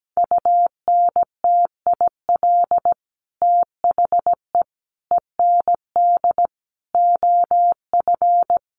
Morse code, heard as a keyed tone, about 17 wpm, 700 Hz. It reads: UNTIL THE END OF
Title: Sets of 5 Words Encore 17wpm